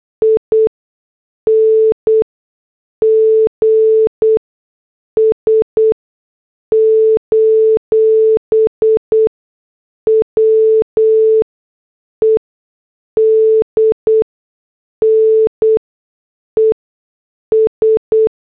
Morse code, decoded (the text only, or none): INGS: WEDNES